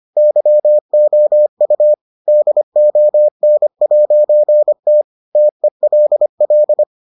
Morse code, heard as a keyed tone, about 25 wpm, 600 Hz. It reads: YOU DON'T TELL